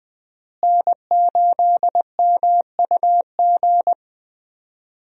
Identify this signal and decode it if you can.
Morse code, heard as a keyed tone, about 20 wpm, 700 Hz. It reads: N8MUG
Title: USA Call Signs - 5 Characters - Repeated 3x - 20wpm